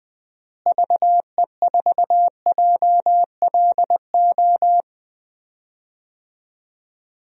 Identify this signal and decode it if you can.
Morse code, heard as a keyed tone, about 20 wpm, 700 Hz. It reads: VE4JLO
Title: DXCC-100 Call Signs - 6 Characters - Repeated 2x - 20wpm